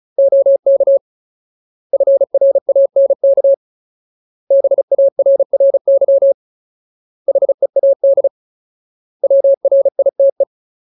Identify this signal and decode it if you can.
Morse code, heard as a keyed tone, about 34 wpm, 550 Hz. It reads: OK FRANK BARRY HEAD WRITE
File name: Ultimate Word List / 35WPM Ultimate